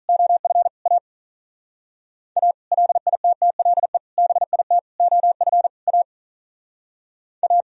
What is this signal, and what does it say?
Morse code, heard as a keyed tone, about 44 wpm, 700 Hz. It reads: OF A A LITTLE BIT OF A A